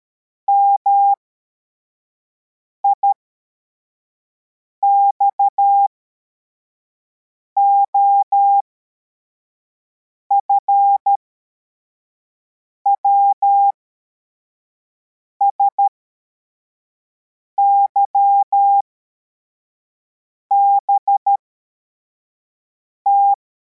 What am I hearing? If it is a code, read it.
Morse code, heard as a keyed tone, about 13 wpm, spaced as for 4 wpm, 800 Hz. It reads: MIXOF WSYBT
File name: formation / cw / Text74